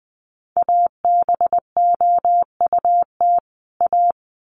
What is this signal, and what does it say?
Morse code, heard as a keyed tone, about 20 wpm, 700 Hz. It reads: ABOUT A
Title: Sets of 2 Words 20wpm